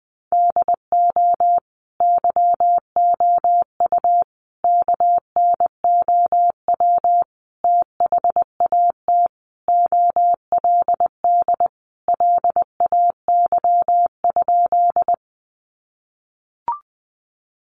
Morse code, heard as a keyed tone, about 20 wpm, 700 Hz. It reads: DO YOU KNOW THAT OLD LAY?
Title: Sentences from Top 400 Words 20wpm